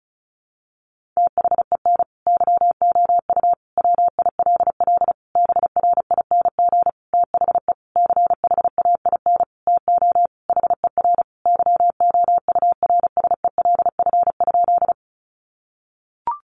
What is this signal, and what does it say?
Morse code, heard as a keyed tone, about 34 wpm, 700 Hz. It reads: THEN YOU WILL BRING THE CHAIN TO HER YOURSELF?
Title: Sentences from Top 1500 Words 35wpm